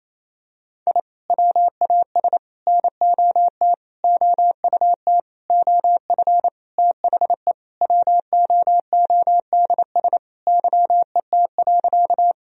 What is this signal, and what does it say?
Morse code, heard as a keyed tone, about 28 wpm, 700 Hz. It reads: I WAS NOT OUT OF THE WOODS YET.
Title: Sentences from Top 1000 Words - Review Part 2 - 28wpm